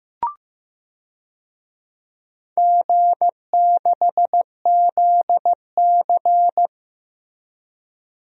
Morse code, heard as a keyed tone, about 15 wpm, 700 Hz. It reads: G6ZC
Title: DXCC-100 Call Signs - 4 Characters - Repeated 2x - 15wpm